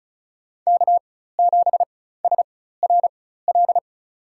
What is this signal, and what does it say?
Morse code, heard as a keyed tone, about 36 wpm, spaced as for 16 wpm, 700 Hz. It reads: K7SRL
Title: USA Call Signs - 4x Character Spacing - 35wpm